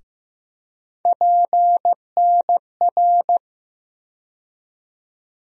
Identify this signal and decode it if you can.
Morse code, heard as a keyed tone, about 15 wpm, 700 Hz. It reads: PNR
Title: USA Call Sign Postfixes - 3 Letters 15wpm